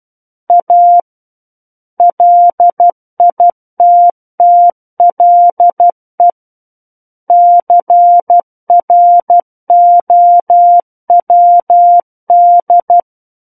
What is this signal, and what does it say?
Morse code, heard as a keyed tone, about 12 wpm, 700 Hz. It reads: A LITTLE CROWD
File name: Book One / 03-Chapter 3 - 12 WPM